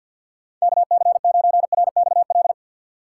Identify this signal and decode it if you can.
Morse code, heard as a keyed tone, about 51 wpm, 700 Hz. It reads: KC9RXL
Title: USA Call Signs - 6 Characters - 4x Character Spacing - 50wpm